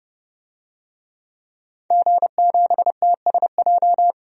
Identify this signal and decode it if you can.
Morse code, heard as a keyed tone, about 30 wpm, 700 Hz. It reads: G7TSJ
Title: DXCC-100 Call Signs - 5 Characters - 30wpm